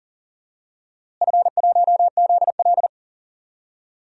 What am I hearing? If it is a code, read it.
Morse code, heard as a keyed tone, about 40 wpm, 700 Hz. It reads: F1ZL
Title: DXCC-100 Call Signs - 4 Characters - Repeated 2x - 40wpm